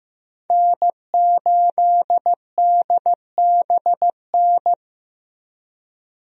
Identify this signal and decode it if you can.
Morse code, heard as a keyed tone, about 15 wpm, 700 Hz. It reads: N8DBN
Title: USA Call Signs - 5 Characters - Repeated 2x - 15wpm